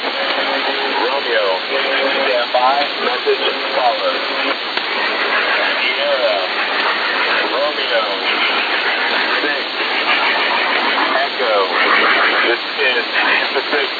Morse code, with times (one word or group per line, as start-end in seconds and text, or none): none